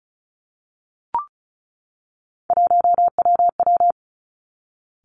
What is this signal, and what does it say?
Morse code, heard as a keyed tone, about 36 wpm, 700 Hz. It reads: 1WW
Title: Three Challenging Letter-Number Combinations 35wpm